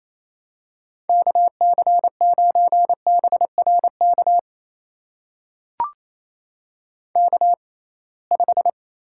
Morse code, heard as a keyed tone, about 28 wpm, spaced as for 21 wpm, 700 Hz. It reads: KC9BRK K5
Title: USA Call Signs - 6x Character Spacing - 28wpm